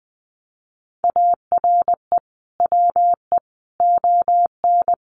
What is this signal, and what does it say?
Morse code, heard as a keyed tone, about 20 wpm, 700 Hz. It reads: ARE WE ON
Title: Sentences from Top 1200 Words 20wpm